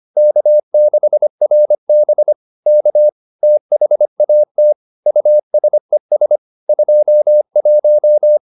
Morse code, heard as a keyed tone, about 25 wpm, 600 Hz. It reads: K6RB K THAT USES 21